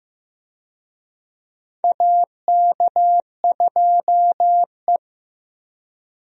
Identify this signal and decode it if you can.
Morse code, heard as a keyed tone, about 15 wpm, 700 Hz. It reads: AK2E
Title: USA Call Signs - 4 Characters - Repeated 2x - 15wpm